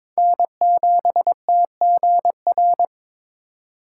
Morse code, heard as a keyed tone, about 26 wpm, 700 Hz. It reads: N7TGR
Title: Speed Racing - USA 1x3 Call Signs 22wpm